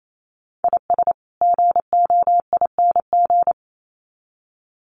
Morse code, heard as a keyed tone, about 28 wpm, 700 Hz. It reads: IS GOING
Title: Sets of 2 Words 28wpm